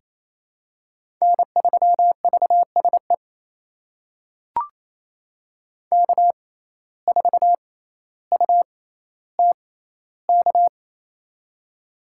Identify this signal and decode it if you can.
Morse code, heard as a keyed tone, about 28 wpm, spaced as for 14 wpm, 700 Hz. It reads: N3VSE K4UTK
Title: USA Call Signs - 5 Characters - 6x Character Spacing - 28wpm